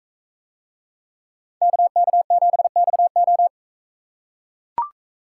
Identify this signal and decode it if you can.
Morse code, heard as a keyed tone, about 44 wpm, 700 Hz. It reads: KK7XO